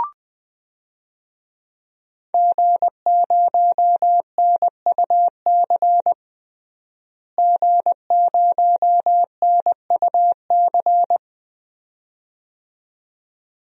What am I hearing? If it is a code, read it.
Morse code, heard as a keyed tone, about 20 wpm, 700 Hz. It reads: G0NUC G0NUC
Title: DXCC-100 Call Signs - 5 Characters - Repeated 2x - 20wpm